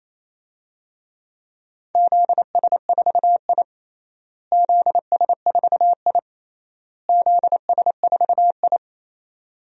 Morse code, heard as a keyed tone, about 28 wpm, 700 Hz. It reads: ZS4I ZS4I ZS4I
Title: DXCC-100 Call Signs - 4 Characters - Repeated 3x - 28wpm